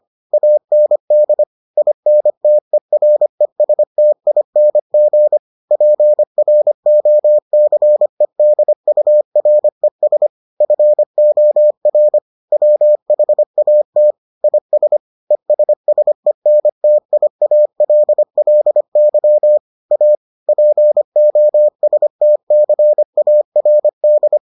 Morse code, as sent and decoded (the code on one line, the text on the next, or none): .- -. -.. / .. -. - . .-. ... - .. -. --. / .--. .-. --- -.-. -.. ..- .-. ... / ..-. --- .-. / .-- .... .- - / .. ... / . ... ... -. - .. .- .-.. .-.. -.-- / .- / .--. --- ... - -.-. .- .-. -..
AND INTERESTING PROCEDURES FOR WHAT IS ESSENTIALLY A POSTCARD